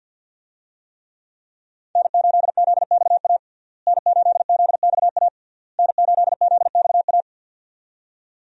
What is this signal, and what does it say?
Morse code, heard as a keyed tone, about 50 wpm, 700 Hz. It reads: N8ZXA N8ZXA N8ZXA
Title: USA Call Signs - 5 Characters - Repeated 3x - 50wpm